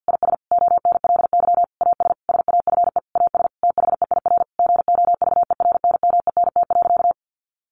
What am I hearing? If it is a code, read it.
Morse code, heard as a keyed tone, about 50 wpm, 700 Hz. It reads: IS ONLY AS SAFE AS THEIR GOVERNMENT.
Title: Sentences from Top 1500 Words 50wpm